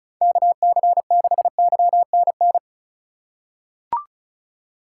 Morse code, heard as a keyed tone, about 34 wpm, 700 Hz. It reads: KC6YNN
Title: USA Call Signs - 3x Character Spacing - 35wpm